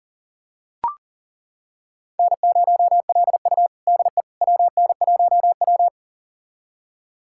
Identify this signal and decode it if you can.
Morse code, heard as a keyed tone, about 40 wpm, 700 Hz. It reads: N0LU DE WN1W
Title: USA Callsign DE Callsign - 4 Characters - 40wpm